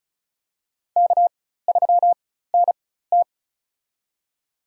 Morse code, heard as a keyed tone, about 34 wpm, spaced as for 16 wpm, 700 Hz. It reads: K3NT
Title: USA Call Signs - 4 Characters - 4x Character Spacing - 35wpm